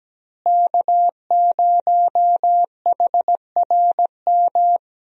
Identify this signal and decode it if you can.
Morse code, heard as a keyed tone, about 17 wpm, 700 Hz. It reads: K0HRM
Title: USA Call Signs - 5 Characters - Repeated 2x - 17wpm